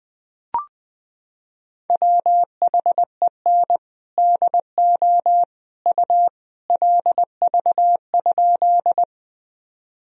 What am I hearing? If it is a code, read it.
Morse code, heard as a keyed tone, about 20 wpm, 700 Hz. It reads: WHEN DO U LV?
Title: Sentences from Top 400 Words and Abbreviations 20wpm